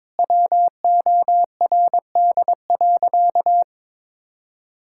Morse code, heard as a keyed tone, about 22 wpm, 700 Hz. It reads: WORD.
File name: practice / 22wpm / Easy Sentences with Hints - 22wpm